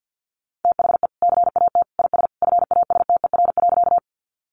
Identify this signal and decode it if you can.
Morse code, heard as a keyed tone, about 50 wpm, 700 Hz. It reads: THE CAT IS FASTER.
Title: Additional Easy Phrases 50wpm